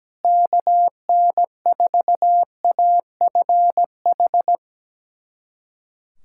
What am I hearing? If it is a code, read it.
Morse code, heard as a keyed tone, about 17 wpm, 700 Hz. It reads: KN4AFH
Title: USA Callsign DE Callsign 17wpm